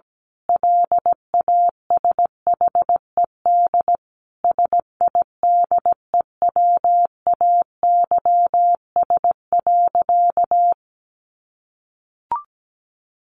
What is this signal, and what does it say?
Morse code, heard as a keyed tone, about 17 wpm, 700 Hz. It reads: LASHED SIDEWAYS.
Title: A Sound Of Thunder 17wpm